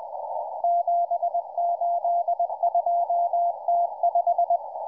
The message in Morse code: 782T5